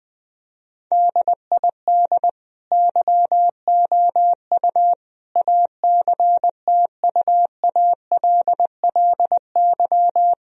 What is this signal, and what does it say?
Morse code, heard as a keyed tone, about 20 wpm, 700 Hz. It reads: DID YOU ACTUALLY
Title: Sentences from Top 1000 Words - Review Part 4 - 20wpm